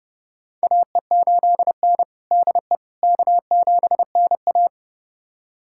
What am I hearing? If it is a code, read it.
Morse code, heard as a keyed tone, about 30 wpm, 700 Hz. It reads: AE8N DE K7NA
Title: USA Callsign DE Callsign - 4 Characters - 30wpm